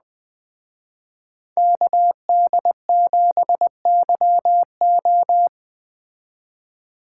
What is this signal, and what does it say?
Morse code, heard as a keyed tone, about 19 wpm, 700 Hz. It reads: KD7YO